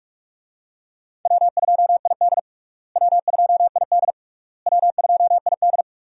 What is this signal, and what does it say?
Morse code, heard as a keyed tone, about 42 wpm, 700 Hz. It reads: W2ID W2ID W2ID
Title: USA Call Signs - 4 Characters - Repeated 3x - 45wpm